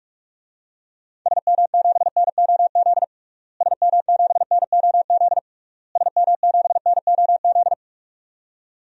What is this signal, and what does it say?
Morse code, heard as a keyed tone, about 46 wpm, 700 Hz. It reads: SM7NOZ SM7NOZ SM7NOZ
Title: DXCC-100 Call Signs - 6 Characters - Repeated 3x - 45wpm